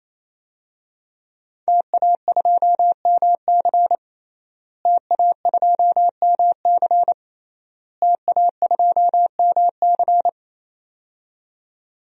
Morse code, heard as a keyed tone, about 28 wpm, 700 Hz. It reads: TA2MC TA2MC TA2MC
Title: DXCC-100 Call Signs - 5 Characters - Repeated 3x - 28wpm